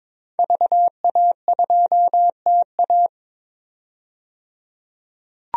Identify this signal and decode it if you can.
Morse code, heard as a keyed tone, about 22 wpm, 700 Hz. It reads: VA2TA
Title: DXCC-100 Call Signs - 5 Characters - 22wpm